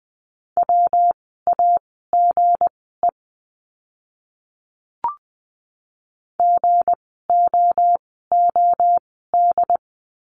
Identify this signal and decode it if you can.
Morse code, heard as a keyed tone, about 20 wpm, spaced as for 14 wpm, 700 Hz. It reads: WAGE GOOD
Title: 160 Lesson 26f - Letter g - Words 20wpm